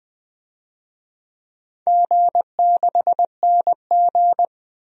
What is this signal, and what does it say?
Morse code, heard as a keyed tone, about 20 wpm, 700 Hz. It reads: G6NG
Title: DXCC-100 Call Signs - 4 Characters - 20wpm